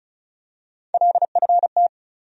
Morse code, heard as a keyed tone, about 36 wpm, 700 Hz. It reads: LFT
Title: USA Call Sign Postfixes - 3 Letters 35wpm